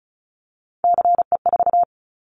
Morse code, heard as a keyed tone, about 36 wpm, 700 Hz. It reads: CE4
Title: Callsign Prefixes for Top 100 DXCC - 35wpm